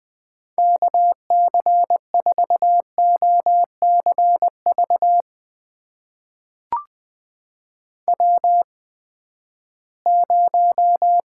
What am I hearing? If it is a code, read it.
Morse code, heard as a keyed tone, about 20 wpm, spaced as for 14 wpm, 700 Hz. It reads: KC4OCV W0
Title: USA Call Signs - 8x Character Spacing - 20wpm